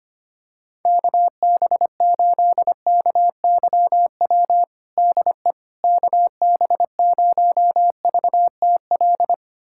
Morse code, heard as a keyed tone, about 25 wpm, 700 Hz. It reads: KB8KYW DE KB0VTL
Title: USA Callsign DE Callsign 25wpm